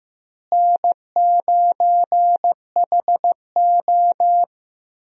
Morse code, heard as a keyed tone, about 15 wpm, 700 Hz. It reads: N9HO
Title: USA Call Signs - 4 Characters - Repeated 2x - 15wpm